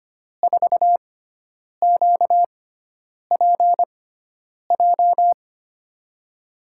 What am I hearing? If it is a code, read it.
Morse code, heard as a keyed tone, about 25 wpm, spaced as for 9 wpm, 700 Hz. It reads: 4QPJ
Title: USA Call Signs - 6 Characters - 6x Character Spacing - 25wpm